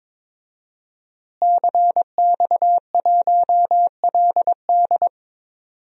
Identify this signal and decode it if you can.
Morse code, heard as a keyed tone, about 22 wpm, 700 Hz. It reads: CX1LD
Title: DXCC-100 Call Signs - 5 Characters - Repeated 3x - 22wpm